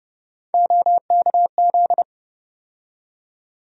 Morse code, heard as a keyed tone, about 30 wpm, 700 Hz. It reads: OKZ
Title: USA Call Sign Postfixes - 3 Letters 30wpm